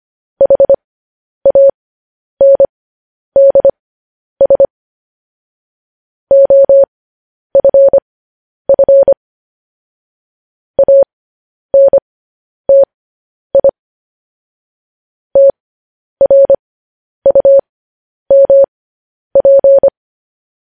Morse code, heard as a keyed tone, about 25 wpm, spaced as for 10 wpm, 550 Hz. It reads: HANDS OFF ANTI TRUMP
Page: news headlines in morse code at 10 wpm